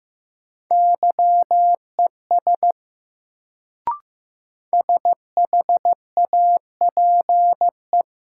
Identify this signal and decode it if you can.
Morse code, heard as a keyed tone, about 15 wpm, 700 Hz. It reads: YES SHAPE
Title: Sentences from Top 500 Words - New Word Drill - 15wpm